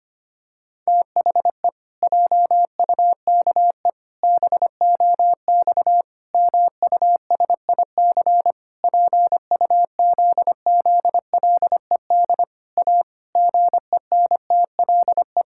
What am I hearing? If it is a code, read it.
Morse code, heard as a keyed tone, about 25 wpm, 700 Hz. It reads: THE JUKE BOX MUSIC PUZZLED A GENTLE
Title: Pangram Challenge v2 - 25wpm